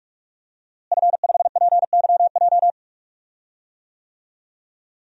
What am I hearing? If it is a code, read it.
Morse code, heard as a keyed tone, about 46 wpm, 700 Hz. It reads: F5PYJ
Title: DXCC-100 Call Signs - 5 Characters - Repeated 2x - 45wpm